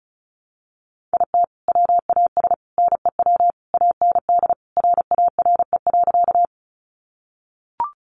Morse code, heard as a keyed tone, about 34 wpm, 700 Hz. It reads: IT WAS NEW AND RARE.
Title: Sentences from Top 1100 Words 35wpm